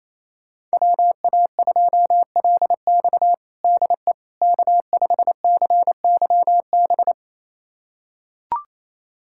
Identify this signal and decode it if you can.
Morse code, heard as a keyed tone, about 28 wpm, 700 Hz. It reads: WA2LX DE K5CYB
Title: USA Callsign DE Callsign - 5 Characters - 28wpm